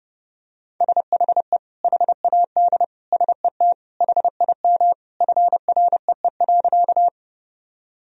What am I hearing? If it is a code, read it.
Morse code, heard as a keyed tone, about 30 wpm, 700 Hz. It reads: SHE HAD SET HIM FREE.
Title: Sentences from Top 1000 Words - Review Part 1 - 30wpm